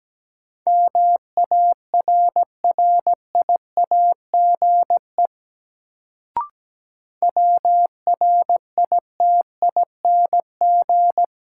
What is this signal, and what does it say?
Morse code, heard as a keyed tone, about 17 wpm, 700 Hz. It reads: MARRIAGE WRITING
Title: Sentences from Top 800 Words - New Word Drill - 17wpm